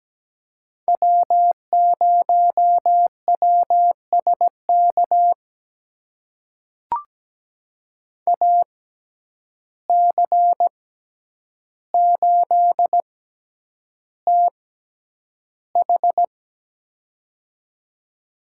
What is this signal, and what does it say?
Morse code, heard as a keyed tone, about 17 wpm, spaced as for 9 wpm, 700 Hz. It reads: W0WSK AC8TH
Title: USA Call Signs - 6x Character Spacing - 17wpm